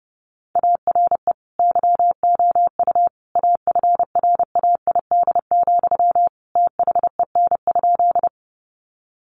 Morse code, heard as a keyed tone, about 30 wpm, 700 Hz. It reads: ARE YOU AFRAID, THEN?
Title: Sentences from Top 1500 Words 30wpm